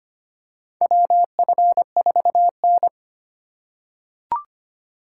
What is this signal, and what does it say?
Morse code, heard as a keyed tone, about 25 wpm, 700 Hz. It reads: WF4N
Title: USA Call Signs - 4 Characters - 4x Character Spacing - 25wpm